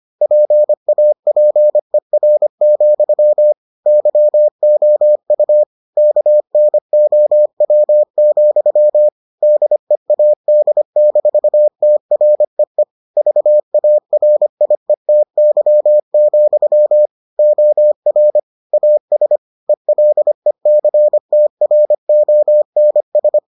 Morse code, heard as a keyed tone, about 25 wpm, 600 Hz. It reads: PAPER, YOU KNOW, DEAD-TREE VARIETY, OR AS ELECTRONS